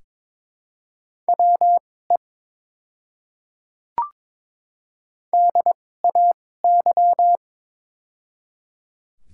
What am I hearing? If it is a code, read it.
Morse code, heard as a keyed tone, about 22 wpm, spaced as for 16 wpm, 700 Hz. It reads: WE DAY
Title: 2x Character Spacing - Top 100 Words - 22wpm